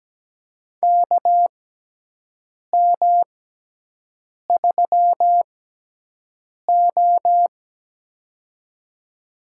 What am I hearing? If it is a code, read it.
Morse code, heard as a keyed tone, about 17 wpm, spaced as for 6 wpm, 700 Hz. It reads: KM3O
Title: USA Call Signs - 4 Characters - 6x Character Spacing - 17wpm